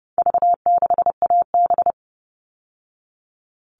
Morse code, heard as a keyed tone, about 30 wpm, 700 Hz. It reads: V6AB
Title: All Cut Numbers - 4-Digit 30wpm